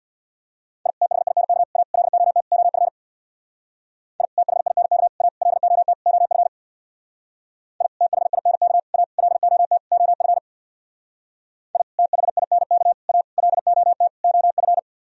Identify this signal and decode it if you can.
Morse code, heard as a keyed tone, about 65 wpm, 700 Hz. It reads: I THINK A LOT OF I THINK A LOT OF I THINK A LOT OF I THINK A LOT OF